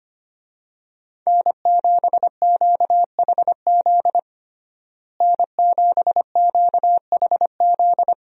Morse code, heard as a keyed tone, about 25 wpm, 700 Hz. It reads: N7QHZ N7QHZ
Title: USA Call Signs - 5 Characters - Repeated 2x - 25wpm